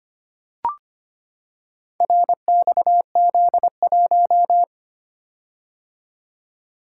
Morse code, heard as a keyed tone, about 25 wpm, 690 Hz. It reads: RXZ1
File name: practice / 25wpm / Four Letter-Number Combinations - 25wpm